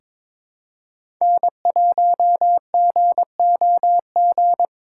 Morse code, heard as a keyed tone, about 22 wpm, 700 Hz. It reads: N1GOG
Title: USA Call Signs - 5 Characters - 2x Character Spacing - 22wpm